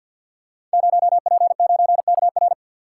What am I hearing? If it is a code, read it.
Morse code, heard as a keyed tone, about 48 wpm, 700 Hz. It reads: 0P9KR
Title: Code Groups - 5 Letter:Numbers - 50wpm